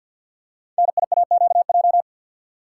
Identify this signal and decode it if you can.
Morse code, heard as a keyed tone, about 50 wpm, 690 Hz. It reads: NIAQJ
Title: Code Groups - 5 Letters - 50wpm